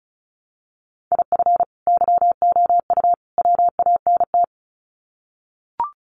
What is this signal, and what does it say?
Morse code, heard as a keyed tone, about 34 wpm, 700 Hz. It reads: IF YOU WANT
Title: Sets of 3 Words 35wpm